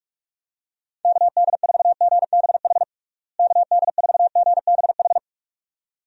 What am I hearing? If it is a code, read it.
Morse code, heard as a keyed tone, about 41 wpm, 700 Hz. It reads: KD4GBH KD4GBH
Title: USA Call Signs - 6 Characters - Repeated 2x - 45wpm